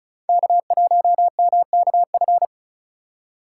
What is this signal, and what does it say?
Morse code, heard as a keyed tone, about 36 wpm, 690 Hz. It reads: K1MKF
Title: USA Call Signs - 5 Characters - 3x Character Spacing - 35wpm